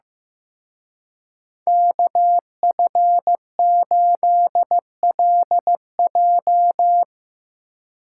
Morse code, heard as a keyed tone, about 17 wpm, 700 Hz. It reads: KF8LJ
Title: Speed Racing - USA 2x2 Call Signs 15wpm